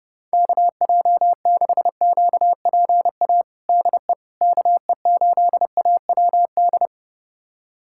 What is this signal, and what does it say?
Morse code, heard as a keyed tone, about 32 wpm, 700 Hz. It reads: KJ6QPA DE KE8AWD
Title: USA Callsign DE Callsign - 6 Characters - 30wpm